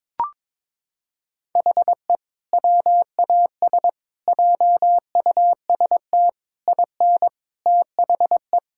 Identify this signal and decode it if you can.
Morse code, heard as a keyed tone, about 22 wpm, 700 Hz. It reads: HE WAS JUST IN THE
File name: Sentences with 7-Word Phrases - 22wpm